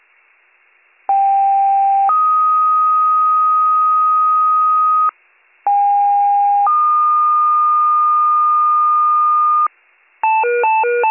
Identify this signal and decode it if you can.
Two-tone page, heard as a one-second tone, then a three-second tone, about 790 then 1250 Hz, about 810 then 1220 Hz. It is heard from 1.1 to 9.7 s.